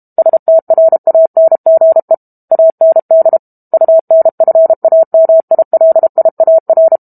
Morse code, heard as a keyed tone, about 32 wpm, 650 Hz. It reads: STRANGE AND UNFAMILIAR
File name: Book Two / 06-Chapter 6 - 32 WPM